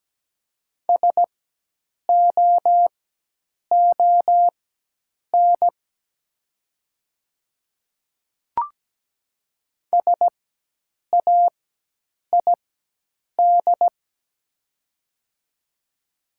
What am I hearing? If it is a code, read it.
Morse code, heard as a keyed tone, about 17 wpm, spaced as for 8 wpm, 700 Hz. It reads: SOON SAID
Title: 4x Character Spacing - Top 500 Words - 17wpm